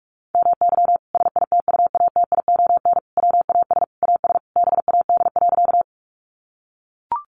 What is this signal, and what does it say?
Morse code, heard as a keyed tone, about 44 wpm, 700 Hz. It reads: MY SITUATION WAS AS BAD.
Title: Sentences from Top 1200 Words 45wpm